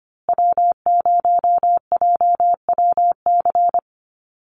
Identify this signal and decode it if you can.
Morse code, heard as a keyed tone, about 25 wpm, 700 Hz. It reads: W0JWC
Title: USA Call Signs Repeated 2x 25wpm